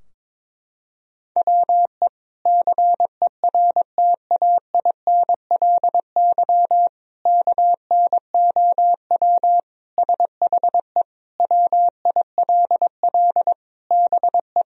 Morse code, heard as a keyed tone, about 22 wpm, 700 Hz. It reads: WE CERTAINLY KNOW SHE WILL BE
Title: Sentences from Top 1000 Words - Review Part 4 - 22wpm